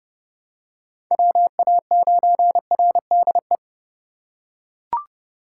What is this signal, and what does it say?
Morse code, heard as a keyed tone, about 29 wpm, 700 Hz. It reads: WA9RDE